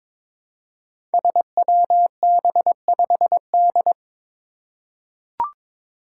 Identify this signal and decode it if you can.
Morse code, heard as a keyed tone, about 22 wpm, 700 Hz. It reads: SWB5D
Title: Five Challenging Letter-Number Combinations 22wpm